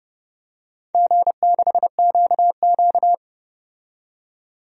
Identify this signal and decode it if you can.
Morse code, heard as a keyed tone, about 30 wpm, 700 Hz. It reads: G6QQ
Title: DXCC-100 Call Signs - 4 Characters - 30wpm